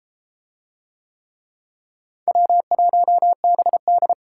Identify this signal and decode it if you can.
Morse code, heard as a keyed tone, about 30 wpm, 700 Hz. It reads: W1BD